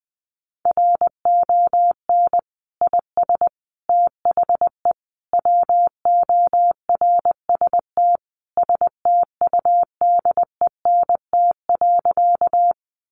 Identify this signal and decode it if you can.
Morse code, heard as a keyed tone, about 20 wpm, 700 Hz. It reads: RON IS THE WORST STUDENT.
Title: Additional Easy Phrases 20wpm